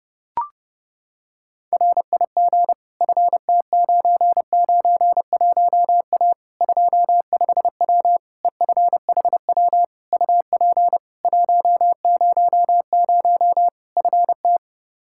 Morse code, heard as a keyed tone, about 30 wpm, 700 Hz. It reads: RIG FT991A 25W EFHW UP 100 FT
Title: Practice QSO - Exchange 2 - Info - 30wpm